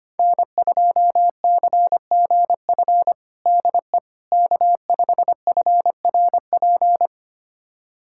Morse code, heard as a keyed tone, about 25 wpm, 700 Hz. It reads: N2CGF DE K5FRP
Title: USA Callsign DE Callsign - 5 Characters - 25wpm